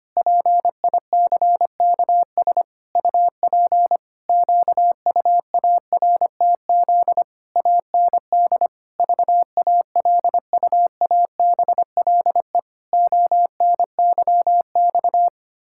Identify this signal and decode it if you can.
Morse code, heard as a keyed tone, about 25 wpm, 700 Hz. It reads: PICKS UP QUARTZ AND VALUABLE ONYX
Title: Pangram Challenge v2 - 25wpm